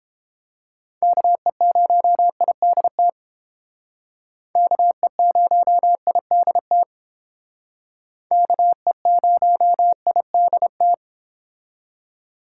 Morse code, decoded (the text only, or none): KE0IDT KE0IDT KE0IDT